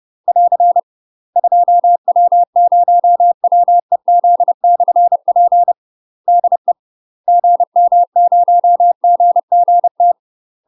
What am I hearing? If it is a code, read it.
Morse code, heard as a keyed tone, about 30 wpm, 700 Hz. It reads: <AR> 2W0WEZ/P DE GM0GGT
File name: UK MASTER QSOs TRAINING / 30WPM / UK MASTER QSOs 0007